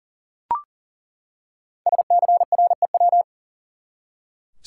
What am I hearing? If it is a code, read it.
Morse code, heard as a keyed tone, about 40 wpm, 700 Hz. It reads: SCREW
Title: Five Letter Words - 40wpm